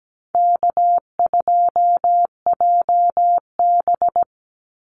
Morse code, heard as a keyed tone, about 17 wpm, 700 Hz. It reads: K2JB
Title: NA SOTA Activator Callsigns 17wpm